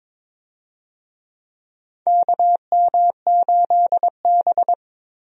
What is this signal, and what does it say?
Morse code, heard as a keyed tone, about 22 wpm, 700 Hz. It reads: KM8B
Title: USA Call Signs - 4 Characters - Repeated 2x - 22wpm